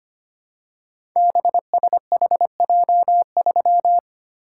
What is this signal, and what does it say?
Morse code, heard as a keyed tone, about 25 wpm, 700 Hz. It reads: BSHJ3
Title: Code Groups - 5 Letter:Numbers - 25wpm